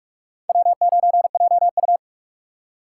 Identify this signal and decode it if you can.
Morse code, heard as a keyed tone, about 46 wpm, 700 Hz. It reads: W9JU
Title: USA Call Signs - 4 Characters - 4x Character Spacing - 45wpm